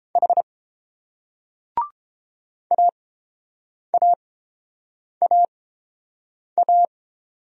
Morse code, heard as a keyed tone, about 30 wpm, spaced as for 7 wpm, 690 Hz. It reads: H AAAA